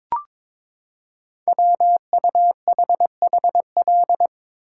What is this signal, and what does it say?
Morse code, heard as a keyed tone, about 22 wpm, 700 Hz. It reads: WUHHL
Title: Code Groups - 5 Letters - 22wpm